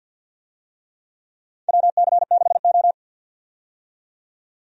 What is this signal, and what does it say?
Morse code, heard as a keyed tone, about 51 wpm, 700 Hz. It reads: WC6O
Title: USA Call Signs - 4 Characters - 50wpm